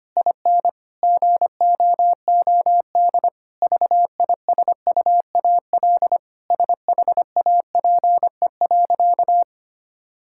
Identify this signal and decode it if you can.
Morse code, heard as a keyed tone, about 25 wpm, 700 Hz. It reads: IN GOOD VISUAL SHAPE.
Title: Sentences from Top 1000 Words - Review Part 6 - 25wpm